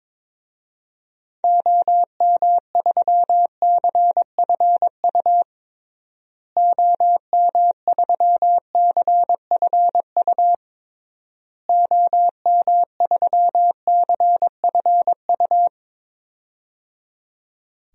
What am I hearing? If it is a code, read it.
Morse code, heard as a keyed tone, about 22 wpm, 700 Hz. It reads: OM3CFU OM3CFU OM3CFU
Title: DXCC-100 Call Signs - 6 Characters - Repeated 3x - 22wpm